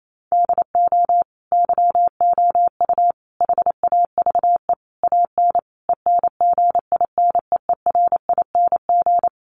Morse code, heard as a keyed tone, about 28 wpm, 700 Hz. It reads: DO YOU HAVE AN ENGINEERING
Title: Sentences from Top 1500 Words 28wpm